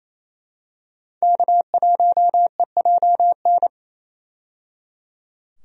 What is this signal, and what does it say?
Morse code, heard as a keyed tone, about 28 wpm, 700 Hz. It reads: K1EJN
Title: USA Call Signs - 5 Characters - 28wpm